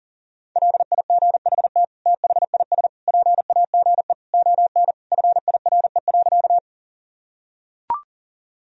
Morse code, heard as a keyed tone, about 40 wpm, 700 Hz. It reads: LIGHT THIS PAGE ON FIRE.
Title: Sentences from Top 1000 Words - Review Part 1 - 40wpm